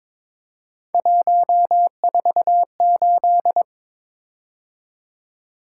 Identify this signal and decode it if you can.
Morse code, heard as a keyed tone, about 23 wpm, 700 Hz. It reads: 148
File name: practice / 22wpm / Three Letter-Number Combinations - 22wpm